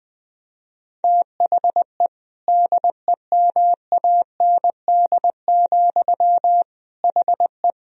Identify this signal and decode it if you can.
Morse code, heard as a keyed tone, about 20 wpm, 700 Hz. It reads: THE DEMAND, HE
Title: Sentences from Top 1000 Words - Review Part 4 - 20wpm